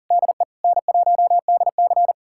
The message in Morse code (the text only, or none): DE N1DC